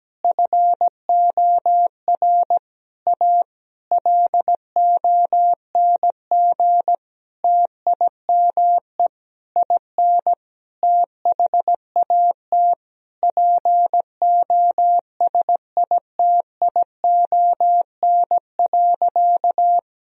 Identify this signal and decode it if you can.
Morse code, heard as a keyed tone, about 17 wpm, 700 Hz. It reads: FOR A LONG TIME IN THAT POSITION.
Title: Sentences from Top 1000 Words - Review Part 2 - 17wpm